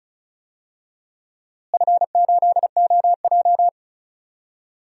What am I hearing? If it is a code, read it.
Morse code, heard as a keyed tone, about 36 wpm, 700 Hz. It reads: F8OJ
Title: DXCC-100 Call Signs - 4 Characters - Repeated 3x - 35wpm